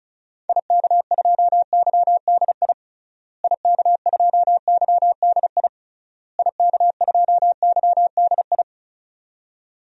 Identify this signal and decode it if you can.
Morse code, heard as a keyed tone, about 36 wpm, 700 Hz. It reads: IK2YDI IK2YDI IK2YDI
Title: DXCC-100 Call Signs - 6 Characters - Repeated 3x - 35wpm